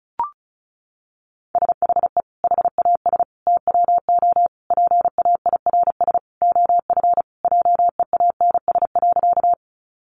Morse code, heard as a keyed tone, about 34 wpm, 700 Hz. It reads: SHE HAS TWO PAIRS OF JEANS.
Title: Additional Easy Phrases 35wpm